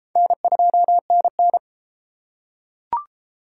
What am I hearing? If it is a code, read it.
Morse code, heard as a keyed tone, about 33 wpm, 700 Hz. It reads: N2NN